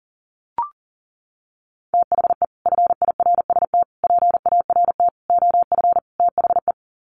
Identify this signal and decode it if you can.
Morse code, heard as a keyed tone, about 40 wpm, 700 Hz. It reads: THE FIRST PART OF THE
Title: Sets of 5 Words Encore 40wpm